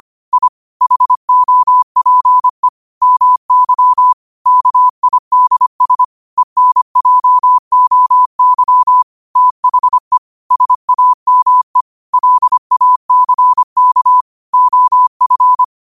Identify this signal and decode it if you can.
Morse code, heard as a keyed tone, about 25 wpm, 1000 Hz. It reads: I HOPE MY KIDS ENJOY THE SAME LACK OF
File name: qotdfiles / 25wpm